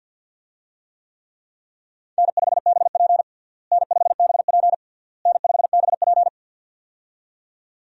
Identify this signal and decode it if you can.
Morse code, heard as a keyed tone, about 51 wpm, 700 Hz. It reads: N5BP N5BP N5BP
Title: USA Call Signs - 4 Characters - Repeated 3x - 50wpm